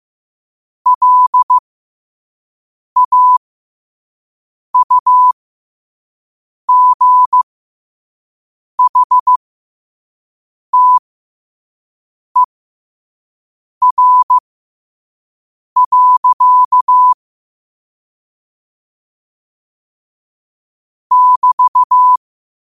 Morse code, heard as a keyed tone, about 15 wpm, spaced as for 5 wpm, 1000 Hz. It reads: LAUGHTER. <BT>
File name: qotdfiles / 5wpm